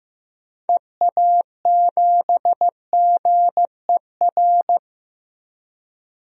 Morse code, heard as a keyed tone, about 15 wpm, 690 Hz. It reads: EA7GER
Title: DXCC-100 Call Signs - 6 Characters - 15wpm